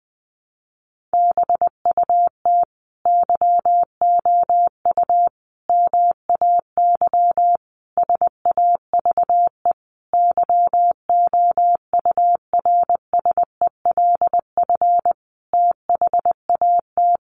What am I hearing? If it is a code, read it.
Morse code, heard as a keyed tone, about 19 wpm, 700 Hz. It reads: BUT YOU MAY SAVE YOURSELF THAT
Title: Sentences from Top 1100 Words 20wpm